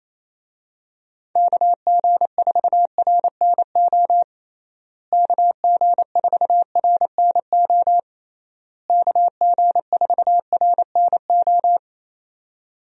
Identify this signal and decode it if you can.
Morse code, heard as a keyed tone, about 27 wpm, 700 Hz. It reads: KG4RNO KG4RNO KG4RNO
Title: USA Call Signs - 6 Characters - Repeated 3x - 28wpm